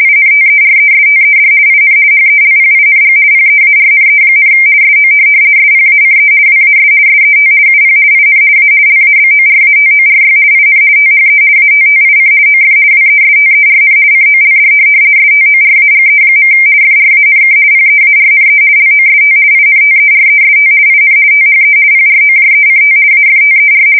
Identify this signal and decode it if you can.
RTTY, about 50 baud. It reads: INTERESTED IN HELPING PEARNE, SO SPACE WAS SET UP IN A LABORATORY IN THE ATTIC OF WESTERN COLD STORAGE. FRANK PEARNE LOST INTEREST IN THE PROJECT AFTER
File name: fsk170-50B / teletype-corp